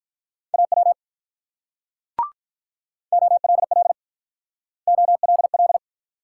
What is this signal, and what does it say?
Morse code, heard as a keyed tone, about 46 wpm, 700 Hz. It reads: AW OLL OLL